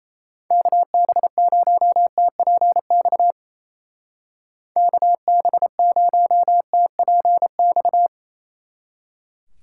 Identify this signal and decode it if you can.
Morse code, heard as a keyed tone, about 34 wpm, 700 Hz. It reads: KB0TPX KB0TPX
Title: Speed Racing - USA 2x3 Call Signs 28wpm